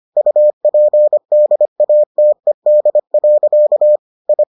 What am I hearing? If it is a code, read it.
Morse code, heard as a keyed tone, about 25 wpm, 600 Hz. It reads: UPDATED. I